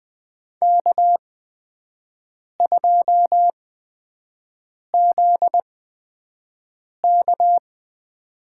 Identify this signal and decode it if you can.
Morse code, heard as a keyed tone, about 20 wpm, spaced as for 5 wpm, 700 Hz. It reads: K2ZK
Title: USA Call Signs - 4 Characters - 8x Character Spacing - 20wpm